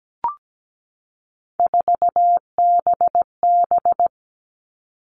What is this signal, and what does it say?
Morse code, heard as a keyed tone, about 17 wpm, 700 Hz. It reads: 4BB
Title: All Cut Numbers - 3-Digit 17wpm